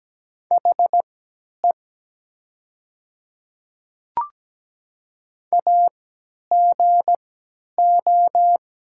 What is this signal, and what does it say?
Morse code, heard as a keyed tone, about 17 wpm, spaced as for 10 wpm, 700 Hz. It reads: HE AGO
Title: 3x Character Spacing - Top 500 Words - 17wpm